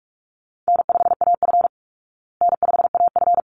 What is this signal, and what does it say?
Morse code, heard as a keyed tone, about 44 wpm, 700 Hz. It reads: N5AF N5AF
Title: Easy Call Signs Repeated 2x - 45wpm